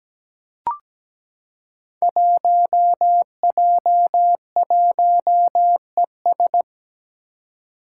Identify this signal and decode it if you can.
Morse code, heard as a keyed tone, about 17 wpm, 700 Hz. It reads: 1J1ES
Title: Five Challenging Letter-Number Combinations 17wpm